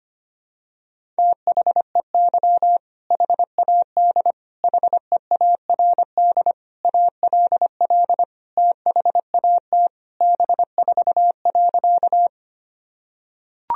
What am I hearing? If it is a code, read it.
Morse code, heard as a keyed tone, about 25 wpm, 700 Hz. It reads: THEY HAD HEARD ALL THAT B4.
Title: Sentences from Top 400 Words and Abbreviations 25wpm